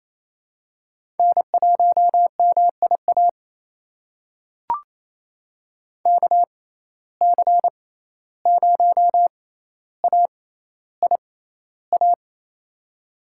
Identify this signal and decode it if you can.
Morse code, heard as a keyed tone, about 28 wpm, spaced as for 14 wpm, 700 Hz. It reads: N1MIA KC0AIA
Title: USA Call Signs - 6x Character Spacing - 28wpm